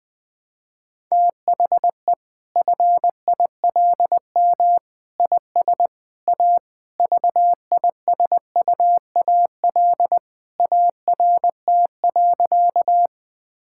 Morse code, heard as a keyed tone, about 21 wpm, 700 Hz. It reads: THE FILM IS A VISUAL ART.
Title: Sentences from Top 1000 Words - Review Part 6 - 20wpm